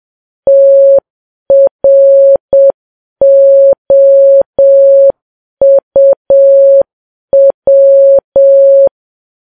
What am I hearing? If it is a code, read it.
Morse code, heard as a keyed tone, about 7 wpm, 550 Hz. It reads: TROUW
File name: Files Web tls / Morse on9fv / morse7kps